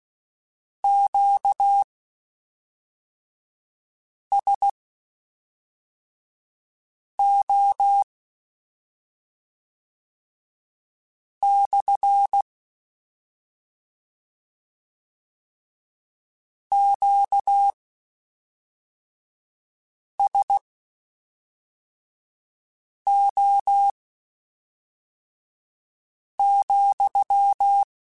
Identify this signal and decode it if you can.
Morse code, heard as a keyed tone, about 16 wpm, spaced as for 3 wpm, 800 Hz. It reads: QSO / QSO,